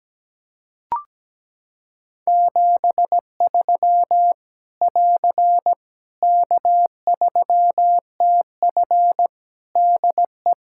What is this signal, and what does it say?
Morse code, heard as a keyed tone, about 17 wpm, 700 Hz. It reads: 73 <AR> K3TF DE
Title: K1USN SST QSO Practice 17wpm